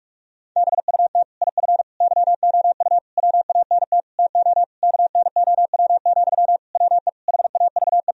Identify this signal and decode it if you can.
Morse code, heard as a keyed tone, about 46 wpm, 700 Hz. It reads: BUT IF YOU WANT TO KNOW, WE HAVE
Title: Sentences from Top 1000 Words - Review Part 3 - 45wpm